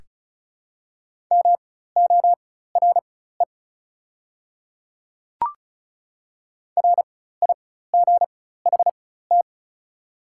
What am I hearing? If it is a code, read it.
Morse code, heard as a keyed tone, about 36 wpm, spaced as for 16 wpm, 690 Hz. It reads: MORE RIGHT
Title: 4x Character Spacing - Top 300 Words - 35wpm